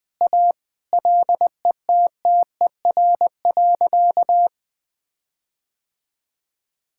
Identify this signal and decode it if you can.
Morse code, heard as a keyed tone, about 20 wpm, 700 Hz. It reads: A LETTER.
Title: Sentences with 5-Word Phrases - 20wpm